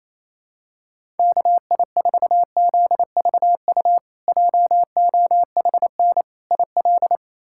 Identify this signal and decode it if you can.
Morse code, heard as a keyed tone, about 28 wpm, 700 Hz. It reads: KI4ZVU JOHN IL
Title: K1USN SST QSO Practice - 28wpm